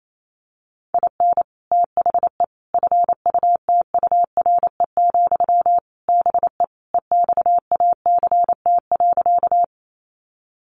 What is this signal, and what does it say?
Morse code, heard as a keyed tone, about 28 wpm, 700 Hz. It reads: IN THE FUTURE, BE EXACT.
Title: Sentences from Top 1300 Words 28wpm